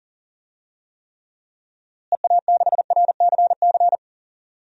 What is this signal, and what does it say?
Morse code, heard as a keyed tone, about 37 wpm, 700 Hz. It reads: EA6RCC
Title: DXCC-100 Call Signs - 6 Characters - Repeated 3x - 40wpm